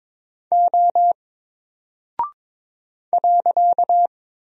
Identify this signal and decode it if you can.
Morse code, heard as a keyed tone, about 22 wpm, 700 Hz. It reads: O .
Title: Single Character (Letter:Number:Punctuation:Prosign) Alt 22wpm